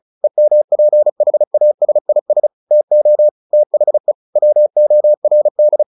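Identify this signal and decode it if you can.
Morse code, heard as a keyed tone, about 34 wpm, 600 Hz. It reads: EMPHASIS TO THE WORD
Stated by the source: Morse practice at 35 wpm from CW med Gnister, sending Ken Follett's novel Never in English.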